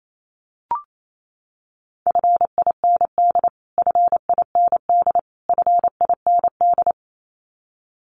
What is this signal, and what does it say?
Morse code, heard as a keyed tone, about 28 wpm, 700 Hz. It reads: FIND FIND FIND
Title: Top 300 Words Repeated 28wpm